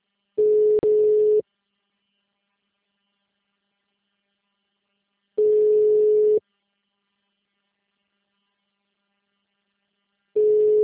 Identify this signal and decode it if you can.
Ringback tone (Europe, 425 Hz).